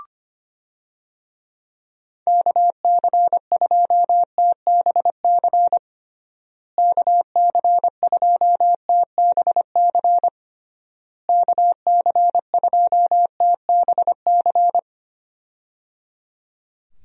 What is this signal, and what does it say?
Morse code, heard as a keyed tone, about 25 wpm, 700 Hz. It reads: KC2TBC KC2TBC KC2TBC
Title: USA Call Signs - 6 Characters - Repeated 3x - 25wpm